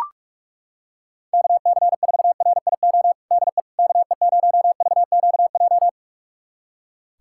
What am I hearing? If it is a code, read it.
Morse code, heard as a keyed tone, about 43 wpm, 700 Hz. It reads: KC4RIO DE KE0VQJ
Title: USA Callsign DE Callsign 45wpm